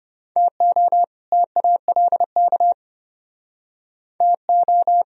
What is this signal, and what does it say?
Morse code, heard as a keyed tone, about 32 wpm, 700 Hz. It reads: TO TALK TO